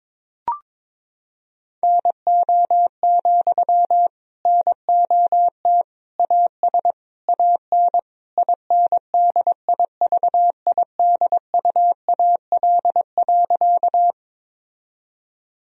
Morse code, heard as a keyed tone, about 22 wpm, 700 Hz. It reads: NO, NOT AS AN INDIVIDUAL.
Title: Sentences from Top 1000 Words - Review Part 4 - 22wpm